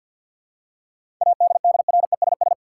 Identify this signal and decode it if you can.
Morse code, heard as a keyed tone, about 47 wpm, 700 Hz. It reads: ADDRESS
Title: Seven Letter Words - 50wpm